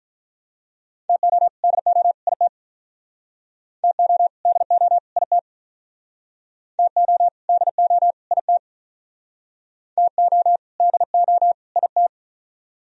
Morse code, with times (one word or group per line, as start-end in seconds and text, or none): 1.1-1.5 TO
1.6-2.1 DO
2.3-2.5 IT
3.8-4.3 TO
4.5-5.0 DO
5.2-5.4 IT
6.8-7.3 TO
7.5-8.1 DO
8.3-8.6 IT
10.0-10.6 TO
10.8-11.5 DO
11.8-12.1 IT